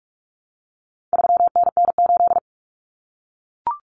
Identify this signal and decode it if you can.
Morse code, heard as a keyed tone, about 44 wpm, 700 Hz. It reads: 3NN8
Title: Common Cut Numbers - 4-Digit 45wpm